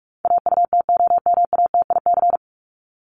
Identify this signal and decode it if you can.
Morse code, heard as a keyed tone, about 44 wpm, 700 Hz. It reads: AUTOMATIC
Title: Top 1751-2000 Words 45wpm